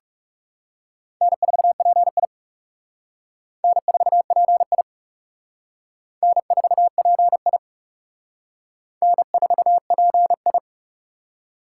Morse code, heard as a keyed tone, about 35 wpm, 700 Hz. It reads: N4PI N4PI N4PI N4PI
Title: Speed Racing - USA 1x2 Call Signs 30wpm